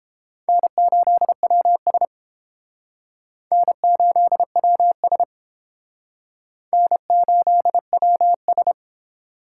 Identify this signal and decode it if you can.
Morse code, heard as a keyed tone, about 28 wpm, 700 Hz. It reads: N8WS N8WS N8WS